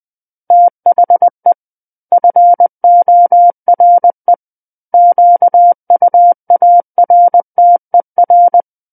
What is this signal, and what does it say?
Morse code, heard as a keyed tone, about 20 wpm, 700 Hz. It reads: THE FORE QUARTER
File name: 20 WPM